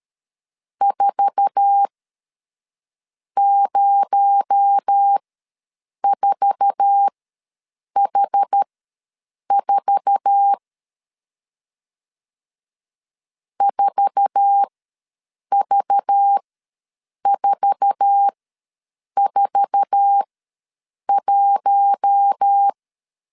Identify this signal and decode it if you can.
Morse code, heard as a keyed tone, about 13 wpm, spaced as for 7 wpm, 800 Hz. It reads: 404H4 4V441
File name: LEZIONI / 8 YJ45 / 8 YJ4 ws 12wpm 3249